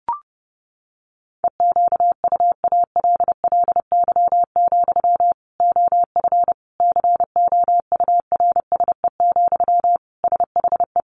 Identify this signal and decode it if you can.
Morse code, heard as a keyed tone, about 30 wpm, 700 Hz. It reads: EQUALLY, OF COURSE, SHE
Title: Sentences from Top 1400 Words 30wpm